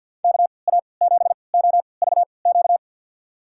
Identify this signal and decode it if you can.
Morse code, heard as a keyed tone, about 51 wpm, spaced as for 28 wpm, 700 Hz. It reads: KA7OVQ
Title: USA Call Signs - 6 Characters - 3x Character Spacing - 50wpm